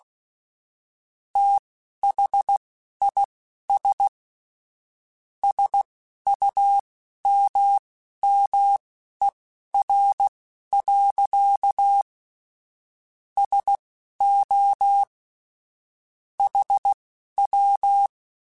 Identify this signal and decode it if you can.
Morse code, heard as a keyed tone, about 16 wpm, spaced as for 12 wpm, 800 Hz. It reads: THIS SUMMER. SO HW
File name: QSOs / QSOs-mp3s / WD0BGP 2@10